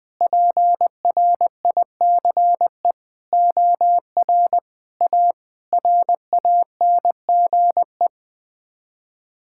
Morse code, heard as a keyed tone, about 20 wpm, 700 Hz. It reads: PRICE OR A RANGE
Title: Sets of 9 Words - 20wpm